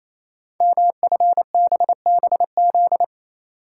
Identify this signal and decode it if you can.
Morse code, heard as a keyed tone, about 29 wpm, 700 Hz. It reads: MFBBZ
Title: Code Groups - 5 Letters - 28wpm